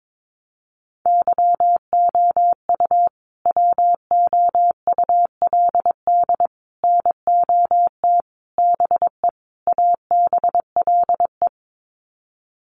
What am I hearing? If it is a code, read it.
Morse code, heard as a keyed tone, about 22 wpm, 700 Hz. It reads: YOU WOULD NOT BE ABLE
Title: Sets of 5 Words Encore2 22wpm